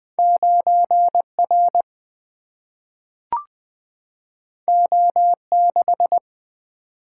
Morse code, heard as a keyed tone, about 20 wpm, 700 Hz. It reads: 9R O6
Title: Single-Letter Single-Number 20wpm